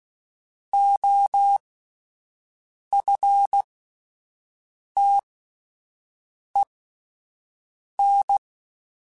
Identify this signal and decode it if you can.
Morse code, heard as a keyed tone, about 16 wpm, spaced as for 5 wpm, 800 Hz. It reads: OFTEN
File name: QSOs Spaced